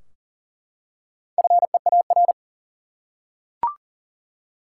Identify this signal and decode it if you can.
Morse code, heard as a keyed tone, about 39 wpm, 700 Hz. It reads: FEAR
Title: Four Letter Words - 40wpm